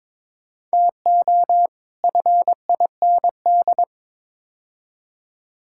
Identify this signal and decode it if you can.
Morse code, heard as a keyed tone, about 22 wpm, 700 Hz. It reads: TO FIND
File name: practice / 22wpm / Build A Phrase - 5-Word Phrases - 22wpm